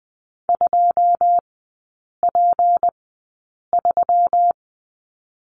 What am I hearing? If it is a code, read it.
Morse code, heard as a keyed tone, about 20 wpm, spaced as for 8 wpm, 700 Hz. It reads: 2P3
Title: 184 Lesson 30e - Letter b - Three Characters 20wpm